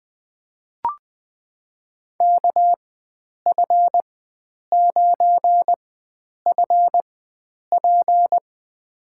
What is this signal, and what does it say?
Morse code, heard as a keyed tone, about 20 wpm, spaced as for 9 wpm, 700 Hz. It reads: KF9FP
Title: USA Call Signs - 5 Characters - 4x Character Spacing - 20wpm